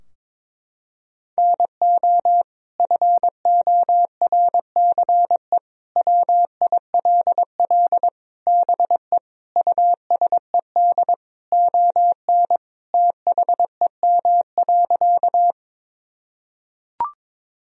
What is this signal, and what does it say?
Morse code, heard as a keyed tone, about 21 wpm, 700 Hz. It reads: NO FORCE WILL BE USED ON THEM.
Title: Sentences from Top 1000 Words - Review Part 3 - 22wpm